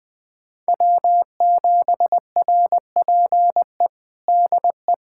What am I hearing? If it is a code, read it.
Morse code, heard as a keyed tone, about 21 wpm, 700 Hz. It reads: W7RPE DE
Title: USA Callsign DE Callsign - 5 Characters - 20wpm